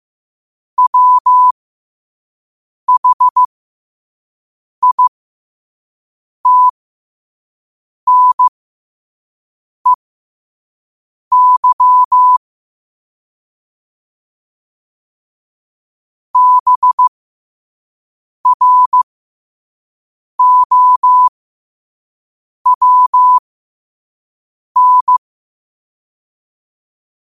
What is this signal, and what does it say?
Morse code, heard as a keyed tone, about 15 wpm, spaced as for 5 wpm, 1000 Hz. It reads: WHITNEY BROWN
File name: qotdfiles / 5wpm